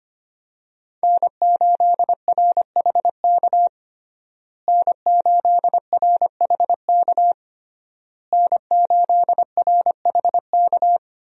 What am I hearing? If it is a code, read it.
Morse code, heard as a keyed tone, about 25 wpm, 700 Hz. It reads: N8RHK N8RHK N8RHK
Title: USA Call Signs - 5 Characters - Repeated 3x - 25wpm